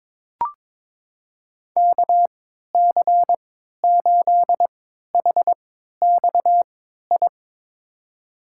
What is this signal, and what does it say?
Morse code, heard as a keyed tone, about 21 wpm, spaced as for 12 wpm, 700 Hz. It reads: KC8HXI
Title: USA Call Signs - 3x Character Spacing - 22wpm